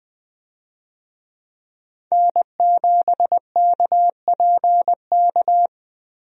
Morse code, heard as a keyed tone, about 20 wpm, 700 Hz. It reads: N7KPK
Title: USA Call Signs - 5 Characters - Repeated 2x - 20wpm